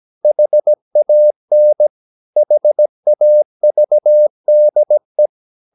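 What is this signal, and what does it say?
Morse code, heard as a keyed tone, about 17 wpm, 600 Hz. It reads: HAN HAVDE
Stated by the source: Slow (QRS) Morse practice at 17 wpm, from CW med Gnister.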